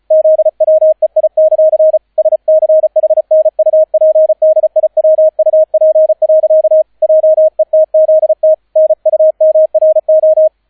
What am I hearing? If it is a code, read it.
Morse code, heard as a keyed tone, about 34 wpm, 630 Hz, with radio noise. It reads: ZWEI; SCHNUPDIWUP. JETZT NUMRO